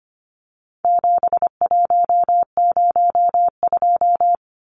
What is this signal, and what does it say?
Morse code, heard as a keyed tone, about 25 wpm, 700 Hz. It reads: 7102
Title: Numbers - 4-Digit 25wpm